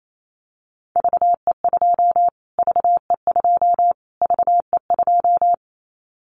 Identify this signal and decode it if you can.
Morse code, heard as a keyed tone, about 28 wpm, 700 Hz. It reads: VE2 VE2 VE2
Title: Callsign Prefixes for Top 25 DXCC - Repeated 3x - 28wpm